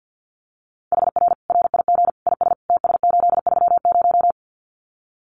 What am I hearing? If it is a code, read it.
Morse code, heard as a keyed tone, about 50 wpm, 700 Hz. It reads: HR RIG IS TS830
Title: Common QSO Phrases 50wpm